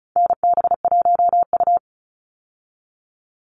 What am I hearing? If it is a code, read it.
Morse code, heard as a keyed tone, about 34 wpm, 700 Hz. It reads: NB1U
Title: Call Signs Repeated 3x - 35wpm